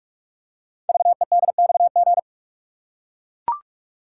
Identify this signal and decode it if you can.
Morse code, heard as a keyed tone, about 44 wpm, 700 Hz. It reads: VEDXG
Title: Code Groups - 5 Letters - 45wpm